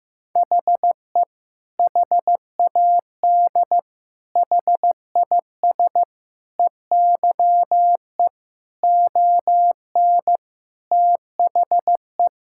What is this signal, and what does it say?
Morse code, heard as a keyed tone, about 15 wpm, 700 Hz. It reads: HE HAD HIS EYE ON THE
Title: Sentences from Top 1000 Words - Review Part 5 - 15wpm